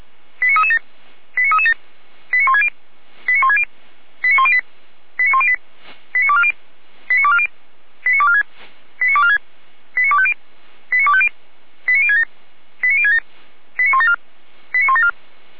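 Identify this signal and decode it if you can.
Five-tone selective call (ZVEI), heard as quick runs of five tones, tones of about 70 ms. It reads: 89337 89337 89160 89160 89108 89108 89240 89240 89246 89246 89250 89250 89067 89067 89174 89174
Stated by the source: ZVEI five-tone selective call 89075 received on SDR.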